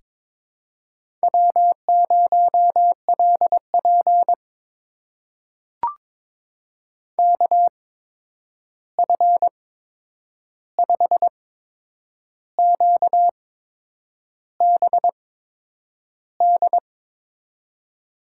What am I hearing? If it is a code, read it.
Morse code, heard as a keyed tone, about 22 wpm, spaced as for 8 wpm, 700 Hz. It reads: W0LP KF5QBD
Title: USA Call Signs - 8x Character Spacing - 22wpm